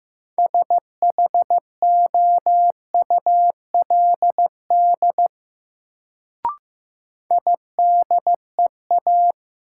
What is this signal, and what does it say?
Morse code, heard as a keyed tone, about 15 wpm, 700 Hz. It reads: SHOULD IDEA
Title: Sentences from Top 300 Words - New Word Drill - 15wpm